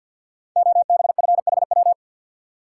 Morse code, heard as a keyed tone, about 51 wpm, 700 Hz. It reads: OBFHW
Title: Code Groups - 5 Letters - 50wpm